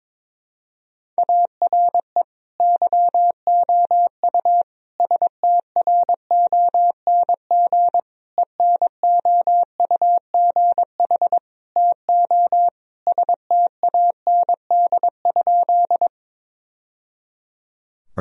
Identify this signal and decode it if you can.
Morse code, heard as a keyed tone, about 22 wpm, 700 Hz. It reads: ARE YOU STRONG ENOUGH TO STAND?
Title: Sentences from Top 1000 Words - Review Part 3 - 22wpm